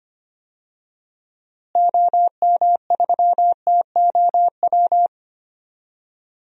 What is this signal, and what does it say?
Morse code, heard as a keyed tone, about 25 wpm, 700 Hz. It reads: OM3TOW
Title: DXCC-100 Call Signs - 6 Characters - Repeated 2x - 25wpm